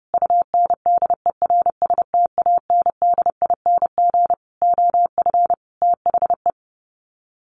Morse code, heard as a keyed tone, about 30 wpm, 700 Hz. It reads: UNDERSTANDING OF THE
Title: Sets of 5 Words Encore2 30wpm